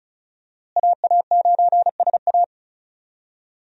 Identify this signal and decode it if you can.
Morse code, heard as a keyed tone, about 34 wpm, 700 Hz. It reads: AA9SA
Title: USA Call Signs - 5 Characters - Repeated 2x - 35wpm